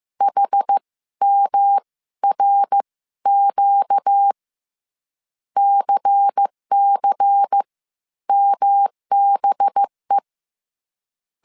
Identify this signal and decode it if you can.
Morse code, heard as a keyed tone, about 15 wpm, spaced as for 13 wpm, 800 Hz. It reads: HMRQ CC MBE
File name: CW / DET / 1 9 1500C S C 14wpm 3245